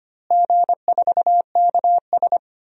Morse code, heard as a keyed tone, about 25 wpm, 700 Hz. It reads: G4KS
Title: DXCC-100 Call Signs - 4 Characters - Repeated 2x - 25wpm